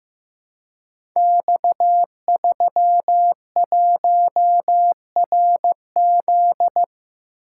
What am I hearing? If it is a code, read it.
Morse code, heard as a keyed tone, about 15 wpm, 700 Hz. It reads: X31RZ
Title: Code Groups - 5 Letter:Numbers - 15wpm